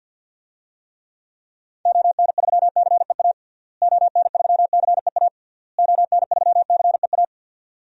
Morse code, heard as a keyed tone, about 52 wpm, 700 Hz. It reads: ON3CEA ON3CEA ON3CEA
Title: DXCC-100 Call Signs - 6 Characters - Repeated 3x - 50wpm